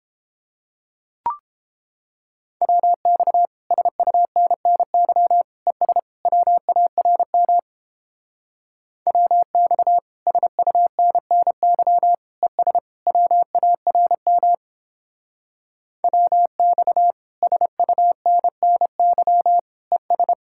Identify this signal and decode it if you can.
Morse code, heard as a keyed tone, about 29 wpm, 700 Hz. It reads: WX SUNNY ES WARM WX SUNNY ES WARM WX SUNNY ES